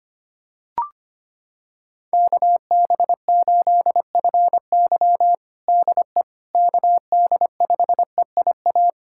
Morse code, heard as a keyed tone, about 25 wpm, 700 Hz. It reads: KB8FY DE KD5EIA
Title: USA Callsign DE Callsign 25wpm